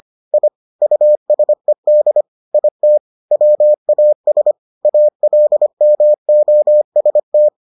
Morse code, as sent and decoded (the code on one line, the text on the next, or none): .. / ..- ... -.. / .. - / .-- .- ... / .- .-.. -- --- ... -
I USED IT WAS ALMOST